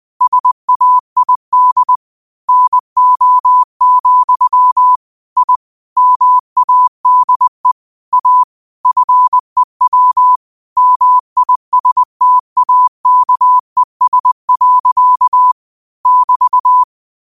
Morse code, heard as a keyed tone, about 20 wpm, 1000 Hz. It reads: SAID NO, I MADE A FEW MISTAKES. <BT>
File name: qotdfiles / 20wpm